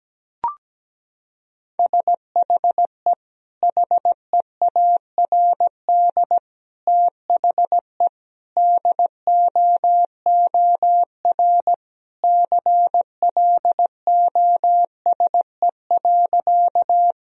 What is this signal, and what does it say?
Morse code, heard as a keyed tone, about 17 wpm, 700 Hz. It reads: SHE HEARD THE DOOR CLOSE.
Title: Sentences from Top 1000 Words - Review Part 1 - 17wpm